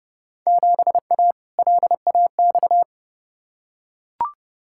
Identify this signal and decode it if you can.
Morse code, heard as a keyed tone, about 29 wpm, 700 Hz. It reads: 7A LAX
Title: Field Day Exchanges - 30wpm